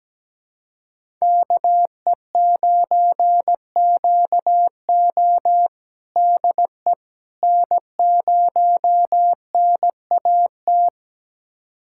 Morse code, heard as a keyed tone, about 17 wpm, 700 Hz. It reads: KE9QO DE N0NAT
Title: USA Callsign DE Callsign - 5 Characters - 17wpm